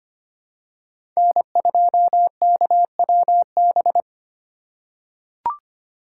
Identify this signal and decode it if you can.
Morse code, heard as a keyed tone, about 25 wpm, 700 Hz. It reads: N2KWB
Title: USA Call Signs - 5 Characters - 3x Character Spacing - 25wpm